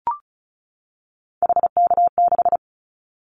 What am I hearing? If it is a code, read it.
Morse code, heard as a keyed tone, about 34 wpm, 700 Hz. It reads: HK6
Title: Callsign Prefixes for Top 100 DXCC - 35wpm